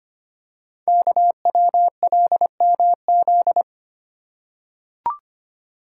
Morse code, heard as a keyed tone, about 25 wpm, 700 Hz. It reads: KWLMZ
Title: Code Groups - 5 Letter:Numbers - 25wpm